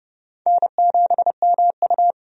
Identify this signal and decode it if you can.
Morse code, heard as a keyed tone, about 30 wpm, 700 Hz. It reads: N7MU
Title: USA Call Signs - 4 Characters - 30wpm